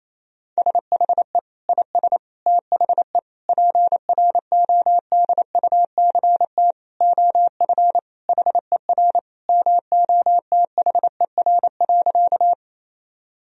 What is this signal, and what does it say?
Morse code, heard as a keyed tone, about 27 wpm, 700 Hz. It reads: SHE IS THE PRODUCT OF HER MOTHER.
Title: Sentences from Top 1000 Words - Review Part 2 - 28wpm